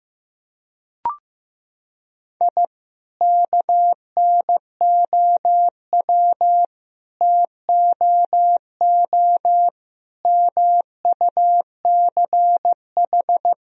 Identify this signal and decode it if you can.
Morse code, heard as a keyed tone, about 15 wpm, 700 Hz. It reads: I KNOW TOO MUCH
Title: Sentences from Top 1000 Words - Review Part 3 - 15wpm